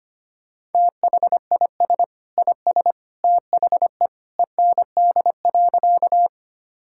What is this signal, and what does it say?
Morse code, heard as a keyed tone, about 25 wpm, 700 Hz. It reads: THIS IS THE END.
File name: practice / 25wpm / Sentences with 2-Word Phrases - 25wpm